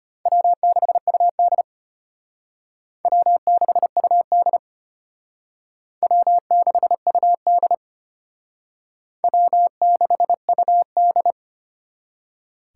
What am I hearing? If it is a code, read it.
Morse code, heard as a keyed tone, about 31 wpm, 700 Hz. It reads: W6UD W6UD W6UD W6UD
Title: Speed Racing - USA 1x2 Call Signs 25wpm